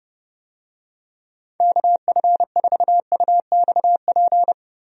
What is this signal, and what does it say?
Morse code, heard as a keyed tone, about 32 wpm, 700 Hz. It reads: KF4UXP
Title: USA Call Signs - 6 Characters - Repeated 3x - 30wpm